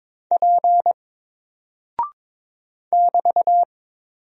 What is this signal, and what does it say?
Morse code, heard as a keyed tone, about 22 wpm, 700 Hz. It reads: P <BT>
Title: Single Character (Letter:Number:Punctuation:Prosign) Alt 22wpm